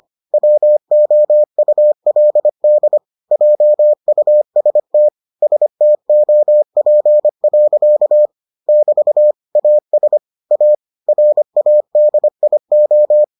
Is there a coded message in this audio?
Yes: Morse code from 0.3 to 13.2 s, heard as a keyed tone, about 25 wpm, 600 Hz. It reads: WOULD JUST STOP. <BT> AS A RADIO